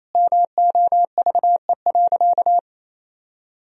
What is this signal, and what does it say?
Morse code, heard as a keyed tone, about 28 wpm, 700 Hz. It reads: MOVE.
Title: Sentences from Top 1000 Words - Review Part 1 - 28wpm